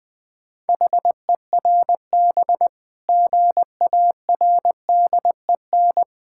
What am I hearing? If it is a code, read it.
Morse code, heard as a keyed tone, about 23 wpm, 700 Hz. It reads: HERB GARDEN
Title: Difficult Words - Speed-Racing with Hints - 20wpm